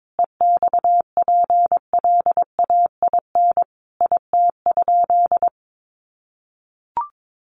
Morse code, heard as a keyed tone, about 22 wpm, 700 Hz. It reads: EXPLAIN IT?
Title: Sentences from Top 1100 Words 22wpm